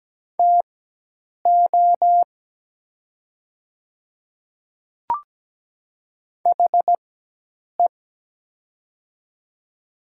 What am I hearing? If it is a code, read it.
Morse code, heard as a keyed tone, about 17 wpm, spaced as for 8 wpm, 700 Hz. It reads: TO HE
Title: 4x Character Spacing - Top 200 Words - 17wpm